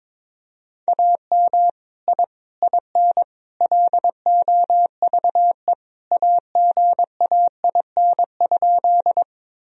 Morse code, heard as a keyed tone, about 22 wpm, 700 Hz. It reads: AM I IN LOVE AGAIN?
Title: Sentences from Top 1000 Words - Review Part 2 - 22wpm